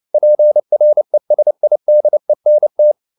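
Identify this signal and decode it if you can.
Morse code, heard as a keyed tone, about 29 wpm, 600 Hz. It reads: PRESIDENT
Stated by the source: Morse code at 29 wpm, sending Ken Follett's novel Never in English, from CW med Gnister.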